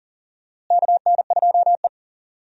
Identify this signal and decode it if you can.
Morse code, heard as a keyed tone, about 40 wpm, 700 Hz. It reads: KN2E
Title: USA Call Signs - 4 Characters - Repeated 3x - 40wpm